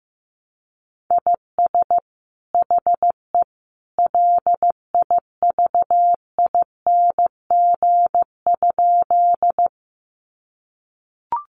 Morse code, heard as a keyed tone, about 15 wpm, 700 Hz. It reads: IS HE LIVING?
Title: Sentences from Top 1100 Words 15wpm